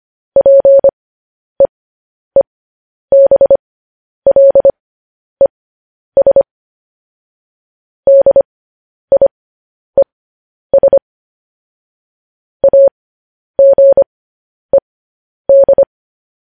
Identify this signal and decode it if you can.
Morse code, heard as a keyed tone, about 25 wpm, spaced as for 10 wpm, 550 Hz. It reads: PEEBLES DIES AGED
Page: news-headlines-in-morse-code-at-10-wpm